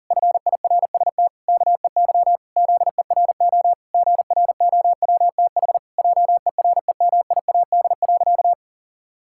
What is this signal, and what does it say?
Morse code, heard as a keyed tone, about 40 wpm, 700 Hz. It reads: FIRST KEY ZERO GROWTH JEREMIAD.
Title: Pangram Challenge v2 - 40wpm